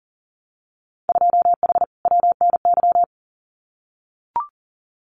Morse code, heard as a keyed tone, about 40 wpm, 700 Hz. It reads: 2H WNY
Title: Winter Field Day Exchanges Encore 40wpm